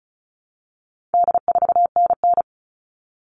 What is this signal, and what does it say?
Morse code, heard as a keyed tone, about 34 wpm, 700 Hz. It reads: D4NN
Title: All Cut Numbers - 4-Digit 35wpm